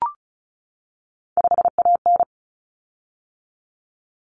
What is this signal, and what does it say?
Morse code, heard as a keyed tone, about 34 wpm, 700 Hz. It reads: 5AN
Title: Common Cut Numbers - 3-Digit 35wpm